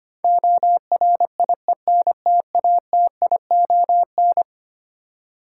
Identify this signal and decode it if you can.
Morse code, heard as a keyed tone, about 25 wpm, 700 Hz. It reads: ORIENTATION
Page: morse-code-ninja-practice-25wpm